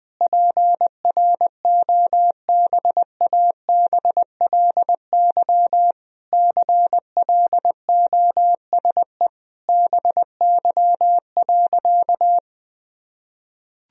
Morse code, heard as a keyed tone, about 21 wpm, 700 Hz. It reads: PROBABLY CLOSE BY.
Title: Sentences from Top 1000 Words - Review Part 4 - 20wpm